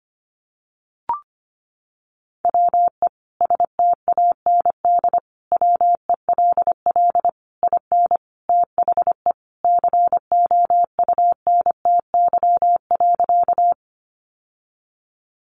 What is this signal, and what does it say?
Morse code, heard as a keyed tone, about 25 wpm, 700 Hz. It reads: WE STAND WELL IN THE COUNTY.
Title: Sentences from Top 1200 Words 25wpm